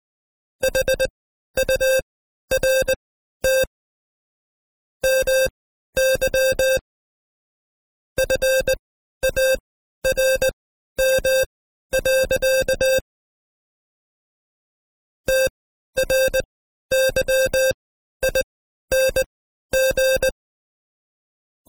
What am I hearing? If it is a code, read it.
Morse code, heard as a keyed tone, about 20 wpm, spaced as for 12 wpm, 530 Hz. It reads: HURT MY FARM. TRYING
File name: Code Quick / AudioDownld2 / Seg132-Stepper5 11WPM